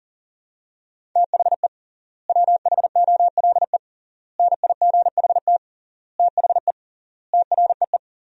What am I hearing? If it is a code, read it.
Morse code, heard as a keyed tone, about 40 wpm, 700 Hz. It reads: THE WHOLE NIGHT THE TREE
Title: 3x Word-Spacing - Sentences from Top 500 Words - 40wpm